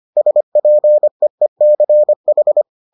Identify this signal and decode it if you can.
Morse code, heard as a keyed tone, about 25 wpm, 600 Hz. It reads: SPEECH